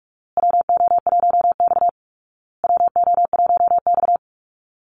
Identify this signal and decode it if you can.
Morse code, heard as a keyed tone, about 44 wpm, 700 Hz. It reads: WO1X WO1X
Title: Call Signs Repeated 2x - 45wpm